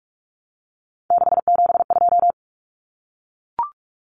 Morse code, heard as a keyed tone, about 44 wpm, 700 Hz. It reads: 672
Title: Numbers - 3-Digit 45wpm